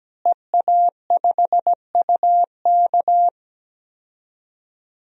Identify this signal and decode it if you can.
Morse code, heard as a keyed tone, about 17 wpm, 700 Hz. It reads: EA5UK
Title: DXCC-100 Call Signs - 5 Characters - Repeated 3x - 17wpm